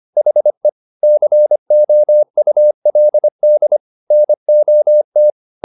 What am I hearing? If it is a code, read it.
Morse code, heard as a keyed tone, about 25 wpm, 600 Hz. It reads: HE COULD NOT